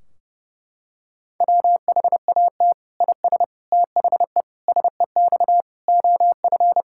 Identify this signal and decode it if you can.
Morse code, heard as a keyed tone, about 30 wpm, 700 Hz. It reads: WHAT IS THE SEX OF
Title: Sentences from Top 1600 Words 30wpm